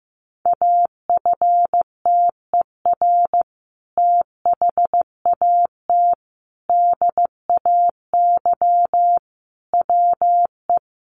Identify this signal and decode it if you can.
Morse code, heard as a keyed tone, about 15 wpm, 700 Hz. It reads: AFTER THAT DAY WE
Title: Sentences from Top 1100 Words 15wpm